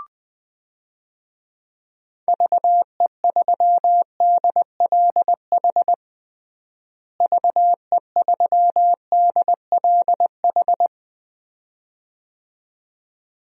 Morse code, heard as a keyed tone, about 20 wpm, 700 Hz. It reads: VE3DLH VE3DLH
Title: DXCC-100 Call Signs - 6 Characters - Repeated 2x - 20wpm